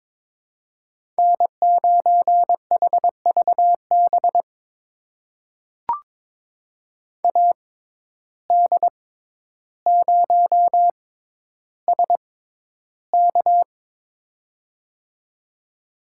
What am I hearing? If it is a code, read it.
Morse code, heard as a keyed tone, about 22 wpm, spaced as for 11 wpm, 700 Hz. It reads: N9HVB AD0SK
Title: USA Call Signs - 5 Characters - 6x Character Spacing - 22wpm